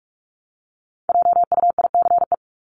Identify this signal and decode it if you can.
Morse code, heard as a keyed tone, about 44 wpm, 700 Hz. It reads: JUICE
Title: Top 1251-1500 Words 45wpm